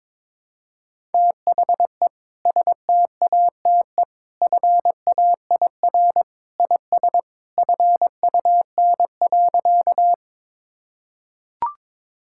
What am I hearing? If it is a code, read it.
Morse code, heard as a keyed tone, about 22 wpm, 700 Hz. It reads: THE STATE FAIR IS FUN.
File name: practice / 22wpm / Sentences with 2-Word Phrases - 22wpm